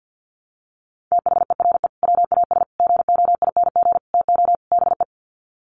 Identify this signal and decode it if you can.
Morse code, heard as a keyed tone, about 50 wpm, 700 Hz. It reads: THERE WAS GOING TO BE
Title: Sets of 5 Words Encore2 50wpm